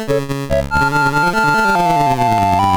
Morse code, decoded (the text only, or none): none